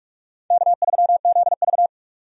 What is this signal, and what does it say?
Morse code, heard as a keyed tone, about 44 wpm, 700 Hz. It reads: K3ZV
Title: USA Call Signs - 4 Characters - 45wpm